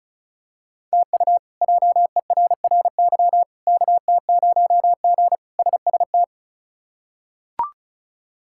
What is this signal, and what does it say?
Morse code, heard as a keyed tone, about 36 wpm, 700 Hz. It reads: TU JERRY KT0G SST
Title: K1USN SST QSO Practice - 35wpm